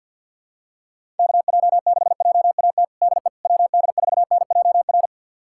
Morse code, heard as a keyed tone, about 51 wpm, 700 Hz. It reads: KJ6JAT DE WD4NJR
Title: USA Callsign DE Callsign - 6 Characters - 50wpm